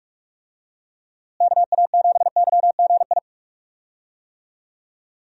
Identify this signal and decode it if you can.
Morse code, heard as a keyed tone, about 46 wpm, 700 Hz. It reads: KA7YGI
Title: USA Call Signs - 6 Characters - Repeated 3x - 45wpm